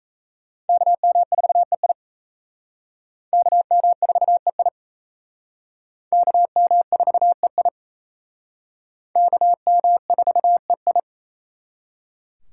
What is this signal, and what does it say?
Morse code, heard as a keyed tone, about 34 wpm, 700 Hz. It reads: KM4EI KM4EI KM4EI KM4EI
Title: Speed Racing - USA 2x2 Call Signs 28wpm